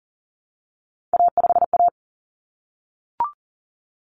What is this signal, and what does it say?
Morse code, heard as a keyed tone, about 40 wpm, 700 Hz. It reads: A5A
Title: Common Cut Numbers - 3-Digit 40wpm